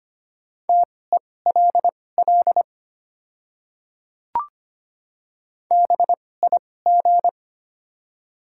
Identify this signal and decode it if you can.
Morse code, heard as a keyed tone, about 25 wpm, spaced as for 18 wpm, 700 Hz. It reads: TELL BIG
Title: 2x Character Spacing - Top 200 Words - 25wpm